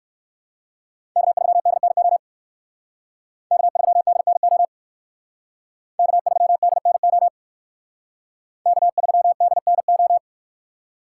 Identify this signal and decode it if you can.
Morse code, heard as a keyed tone, about 58 wpm, 700 Hz. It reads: K3DNO K3DNO K3DNO K3DNO